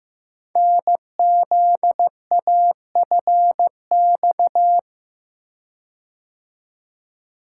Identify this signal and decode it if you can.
Morse code, heard as a keyed tone, about 15 wpm, 700 Hz. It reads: NZAFX
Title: Code Groups - 5 Letters - 15wpm